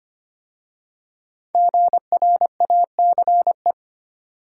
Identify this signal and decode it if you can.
Morse code, heard as a keyed tone, about 25 wpm, 700 Hz. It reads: GRACE
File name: 5 Letter Words - First 3 Letters Given - 25wpm